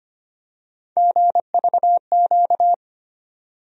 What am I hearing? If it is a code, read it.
Morse code, heard as a keyed tone, about 25 wpm, 700 Hz. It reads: GVQ
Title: USA Call Sign Postfixes - 3 Letters 25wpm